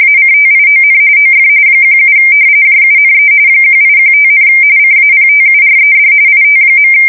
RTTY, about 45.45 baud. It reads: ACTION OF 17 JUNE 1815, CAPTURED IT.
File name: fsk170 / usnavy2